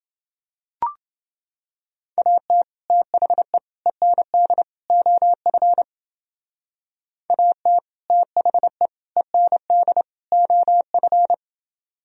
Morse code, heard as a keyed tone, about 28 wpm, 700 Hz. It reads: AT THE END OF AT THE END OF